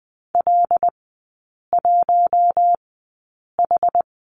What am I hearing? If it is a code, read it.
Morse code, heard as a keyed tone, about 20 wpm, spaced as for 8 wpm, 700 Hz. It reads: L1H
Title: 100 Lesson 17e - Letter u - Three Characters 20wpm